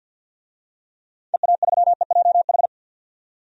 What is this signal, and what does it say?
Morse code, heard as a keyed tone, about 51 wpm, 700 Hz. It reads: EA3EJH
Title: DXCC-100 Call Signs - 6 Characters - 50wpm